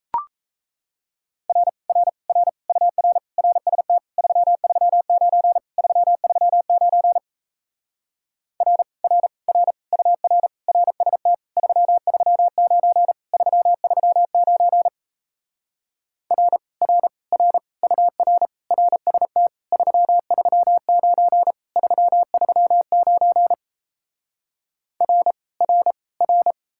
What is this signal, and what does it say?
Morse code, heard as a keyed tone, about 38 wpm, 700 Hz. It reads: R R R UR RST 339 339 R R R UR RST 339 339 R R R UR RST 339 339 R R R